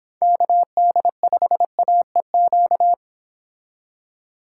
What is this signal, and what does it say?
Morse code, heard as a keyed tone, about 25 wpm, 700 Hz. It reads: KD5AEQ